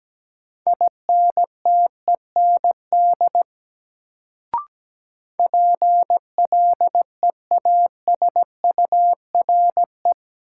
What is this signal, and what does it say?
Morse code, heard as a keyed tone, about 17 wpm, 700 Hz. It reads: INTEND PLEASURE
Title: Sentences from Top 1600 Words - New Word Drill Alt - 17wpm